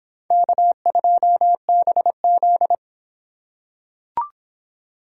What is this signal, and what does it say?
Morse code, heard as a keyed tone, about 22 wpm, 700 Hz. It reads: K2BZ